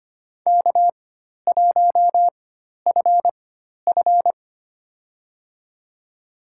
Morse code, heard as a keyed tone, about 25 wpm, spaced as for 12 wpm, 690 Hz. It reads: K1FF
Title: USA Call Signs - 4 Characters - 4x Character Spacing - 25wpm